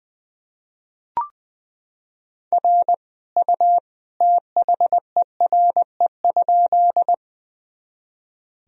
Morse code, heard as a keyed tone, about 21 wpm, 700 Hz. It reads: R U THERE?
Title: Sentences from Top 100 Words and Abbreviations 20wpm